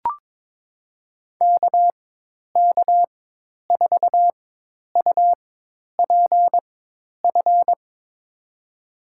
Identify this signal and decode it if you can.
Morse code, heard as a keyed tone, about 21 wpm, spaced as for 10 wpm, 700 Hz. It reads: KK4UPF
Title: USA Call Signs - 6 Characters - 4x Character Spacing - 22wpm